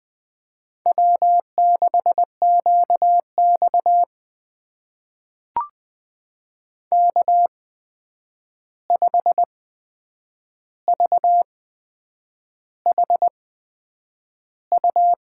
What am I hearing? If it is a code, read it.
Morse code, heard as a keyed tone, about 20 wpm, spaced as for 8 wpm, 700 Hz. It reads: W6QX K5VHU
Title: USA Call Signs - 8x Character Spacing - 20wpm